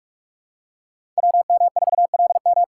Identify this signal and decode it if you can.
Morse code, heard as a keyed tone, about 46 wpm, 700 Hz. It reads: WM4LM
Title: USA Callsign DE Callsign - 5 Characters - 45wpm